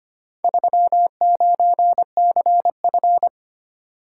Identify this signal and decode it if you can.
Morse code, heard as a keyed tone, about 25 wpm, 690 Hz. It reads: 39CF
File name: practice / 25wpm / Four Letter-Number Combinations - 25wpm